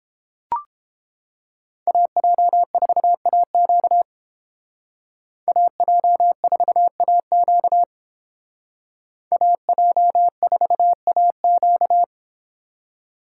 Morse code, heard as a keyed tone, about 28 wpm, 700 Hz. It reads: AJ4AQ AJ4AQ AJ4AQ